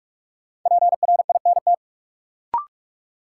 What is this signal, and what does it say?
Morse code, heard as a keyed tone, about 45 wpm, 700 Hz. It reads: PRINT
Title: Sentences from Top 1400 Words - New Word Drill Alt - 45wpm